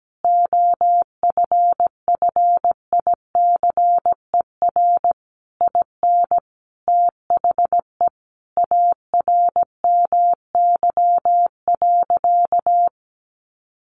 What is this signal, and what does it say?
Morse code, heard as a keyed tone, about 17 wpm, 700 Hz. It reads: OFFICER IN THE ARMY.
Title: Sentences from Top 1200 Words 17wpm